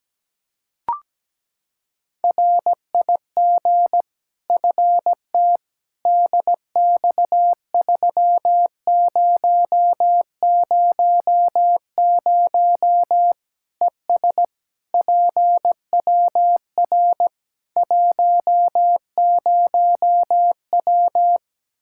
Morse code, heard as a keyed tone, about 17 wpm, 700 Hz. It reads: RIG FT DX3000 ES PWR 10W
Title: Practice QSO - Exchange 2 - Info - 17wpm